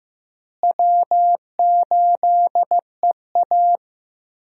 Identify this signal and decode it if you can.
Morse code, heard as a keyed tone, about 15 wpm, 700 Hz. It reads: W8EA
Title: USA Call Signs - 4 Characters - Repeated 3x - 15wpm